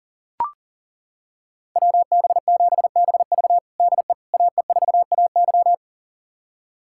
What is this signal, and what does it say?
Morse code, heard as a keyed tone, about 37 wpm, 700 Hz. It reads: WB7BV DE AE4AY
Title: USA Callsign DE Callsign - 5 Characters - 40wpm